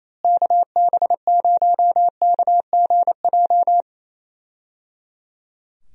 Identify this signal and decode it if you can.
Morse code, heard as a keyed tone, about 28 wpm, 700 Hz. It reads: KB0KGJ
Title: USA Call Signs - 6 Characters - Repeated 3x - 28wpm